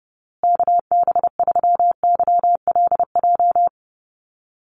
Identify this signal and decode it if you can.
Morse code, heard as a keyed tone, about 30 wpm, 700 Hz. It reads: KB3YLJ
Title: USA Call Signs Repeated 3x 30wpm